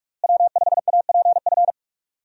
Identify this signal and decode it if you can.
Morse code, heard as a keyed tone, about 42 wpm, 700 Hz. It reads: W5APF
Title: USA Call Signs - 5 Characters - Repeated 3x - 45wpm